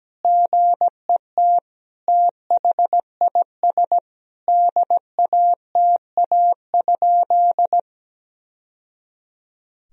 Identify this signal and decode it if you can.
Morse code, heard as a keyed tone, about 17 wpm, 700 Hz. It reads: GET THIS DATA?
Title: Sentences from Top 1000 Words - Review Part 3 - 17wpm